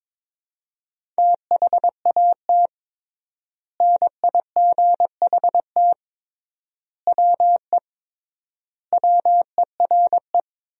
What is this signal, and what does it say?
Morse code, heard as a keyed tone, about 22 wpm, 700 Hz. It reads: THAT NIGHT WE WERE
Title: 3x Word-Spacing - Sentences from Top 300 Words - 22wpm